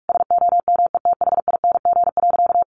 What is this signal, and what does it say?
Morse code, heard as a keyed tone, about 44 wpm, 700 Hz. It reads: SOMETHING.
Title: Sentences from Top 1400 Words 45wpm